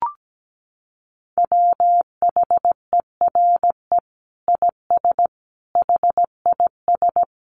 Morse code, heard as a keyed tone, about 17 wpm, 700 Hz. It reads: WHERE IS HIS
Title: Sentences from Top 1100 Words 17wpm